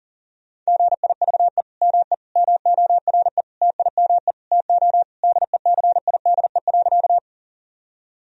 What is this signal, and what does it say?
Morse code, heard as a keyed tone, about 40 wpm, 700 Hz. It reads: GIVE ME MORE TIME TO DECIDE.
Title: Sentences from Top 1000 Words - Review Part 3 - 40wpm